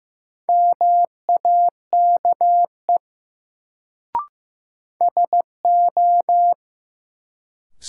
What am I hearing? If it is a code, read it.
Morse code, heard as a keyed tone, about 15 wpm, 700 Hz. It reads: MAKE SO
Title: Sentences from Top 100 Words - New Word Drill - 15wpm